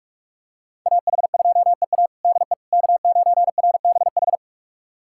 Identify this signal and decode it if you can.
Morse code, heard as a keyed tone, about 47 wpm, 700 Hz. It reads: AH2EA DE K9RBH
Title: USA Callsign DE Callsign - 5 Characters - 45wpm